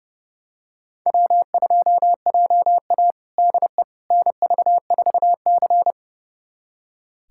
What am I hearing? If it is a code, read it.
Morse code, heard as a keyed tone, about 32 wpm, 700 Hz. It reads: W2JA DE NV4C
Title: USA Callsign DE Callsign - 4 Characters - 30wpm